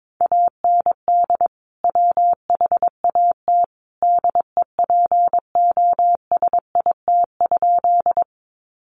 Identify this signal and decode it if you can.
Morse code, heard as a keyed tone, about 22 wpm, 700 Hz. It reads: AND WHAT DEPOSIT?
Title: Sentences from Top 1300 Words 22wpm